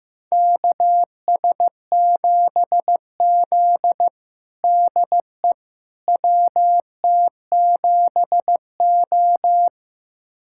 Morse code, heard as a keyed tone, about 15 wpm, 700 Hz. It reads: KS7Z DE WT7O
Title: USA Callsign DE Callsign - 4 Characters - 15wpm